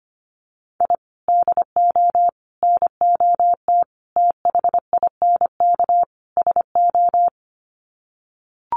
Morse code, heard as a keyed tone, about 25 wpm, 700 Hz. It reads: I DO NOT THINK SO
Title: Sets of 5 Words 25wpm